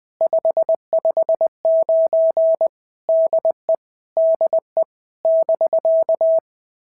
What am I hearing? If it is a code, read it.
Morse code, heard as a keyed tone, about 20 wpm, 650 Hz. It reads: 559 DE DE <BK>
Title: POTA QSO Practice - 20wpm